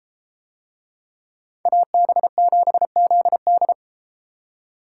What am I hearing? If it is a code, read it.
Morse code, heard as a keyed tone, about 32 wpm, 700 Hz. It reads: AB7ZD